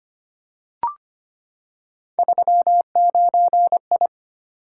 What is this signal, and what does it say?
Morse code, heard as a keyed tone, about 25 wpm, 700 Hz. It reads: 39I
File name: Three Letter-Number Combinations - 25wpm